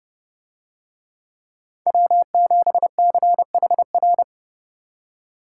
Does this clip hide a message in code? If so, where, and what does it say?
Yes: Morse code from 1.9 to 4.2 s, heard as a keyed tone, about 30 wpm, 700 Hz. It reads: W7CHR